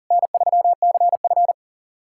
Morse code, heard as a keyed tone, about 41 wpm, 700 Hz. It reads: N3CF
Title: SOTA QSO Practice 40wpm